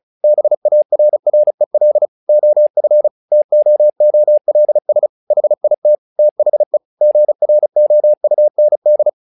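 Morse code, heard as a keyed tone, about 34 wpm, 600 Hz. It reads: BARREL OF TOOLS HIT THE GROUND
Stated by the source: Morse practice text "Tower Trouble" sent at 35 WPM, from the SKCC - Straight Key Century Club learning center audio files.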